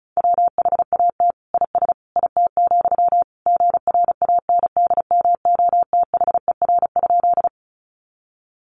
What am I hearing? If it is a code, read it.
Morse code, heard as a keyed tone, about 34 wpm, 700 Hz. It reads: WHAT IS IT, GRANDMOTHER?
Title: Sentences from Top 1200 Words 35wpm